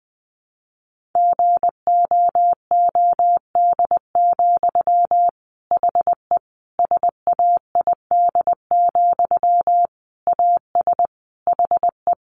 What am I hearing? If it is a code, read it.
Morse code, heard as a keyed tone, about 20 wpm, 700 Hz. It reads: GOOD, HE SAID, AS HE
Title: Sentences from Top 600 Words 20wpm